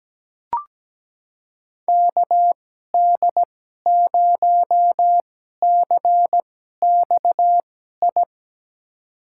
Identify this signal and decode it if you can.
Morse code, heard as a keyed tone, about 17 wpm, spaced as for 12 wpm, 700 Hz. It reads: KD0CXI
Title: USA Call Signs - 2x Character Spacing - 17wpm